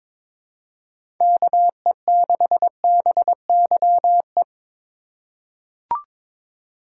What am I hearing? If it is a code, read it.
Morse code, heard as a keyed tone, about 22 wpm, 700 Hz. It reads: KE6BYE
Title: USA Call Signs - 6 Characters - 2x Character Spacing - 22wpm